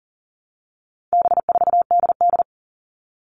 Morse code, heard as a keyed tone, about 40 wpm, 700 Hz. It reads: B4DD
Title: All Cut Numbers - 4-Digit 40wpm